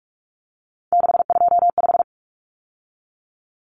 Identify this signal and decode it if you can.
Morse code, heard as a keyed tone, about 44 wpm, 700 Hz. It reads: 625
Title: Common Cut Numbers - 3-Digit 45wpm